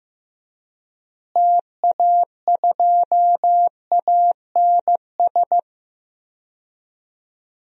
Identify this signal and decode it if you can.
Morse code, heard as a keyed tone, about 15 wpm, 700 Hz. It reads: TA2ANS
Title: DXCC-100 Call Signs - 6 Characters - Repeated 3x - 15wpm